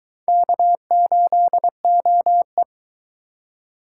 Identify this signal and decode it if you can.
Morse code, heard as a keyed tone, about 23 wpm, 700 Hz. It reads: K8OE